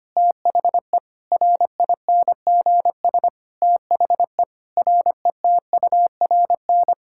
Morse code, heard as a keyed tone, about 25 wpm, 700 Hz. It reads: THE RINGS THE RETURN
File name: Sets of 9 Words - 25wpm